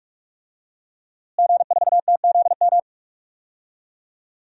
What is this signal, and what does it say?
Morse code, heard as a keyed tone, about 45 wpm, 700 Hz. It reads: G4TZM
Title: DXCC-100 Call Signs - 5 Characters - Repeated 2x - 45wpm